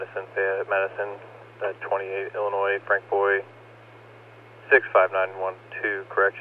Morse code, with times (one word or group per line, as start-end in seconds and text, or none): none